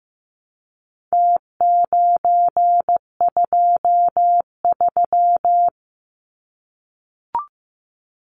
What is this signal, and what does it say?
Morse code, heard as a keyed tone, about 15 wpm, 700 Hz. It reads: T923
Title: Numbers - Contest Style - 4-Digit 15wpm